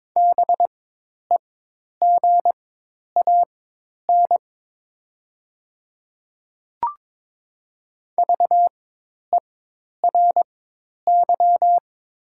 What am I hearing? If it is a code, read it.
Morse code, heard as a keyed tone, about 22 wpm, spaced as for 10 wpm, 700 Hz. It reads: BEGAN VERY
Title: 4x Character Spacing - Top 500 Words - 22wpm